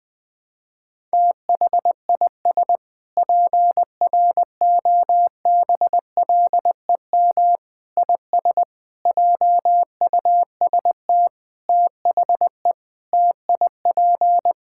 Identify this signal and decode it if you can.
Morse code, heard as a keyed tone, about 21 wpm, 700 Hz. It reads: THIS PROBLEM IS JUST THE TIP
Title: Sentences with 7-Word Phrases - Spoken Beginning - 20wpm